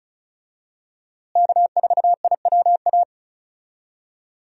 Morse code, heard as a keyed tone, about 36 wpm, 700 Hz. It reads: K4IWA
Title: USA Call Signs - 5 Characters - Repeated 3x - 35wpm